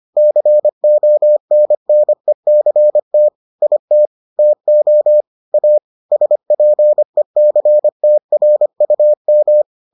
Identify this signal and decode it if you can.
Morse code, heard as a keyed tone, about 25 wpm, 600 Hz. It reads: CONNECT IT TO A SPECTRUM